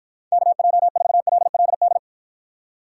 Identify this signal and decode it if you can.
Morse code, heard as a keyed tone, about 52 wpm, 700 Hz. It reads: KJ4LLD